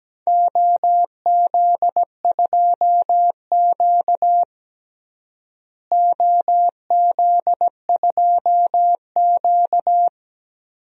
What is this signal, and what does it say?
Morse code, heard as a keyed tone, about 17 wpm, 700 Hz. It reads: OZ2Q OZ2Q
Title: DXCC-100 Call Signs - 4 Characters - Repeated 2x - 17wpm